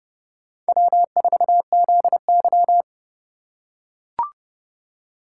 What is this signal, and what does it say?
Morse code, heard as a keyed tone, about 30 wpm, 700 Hz. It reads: W4ZY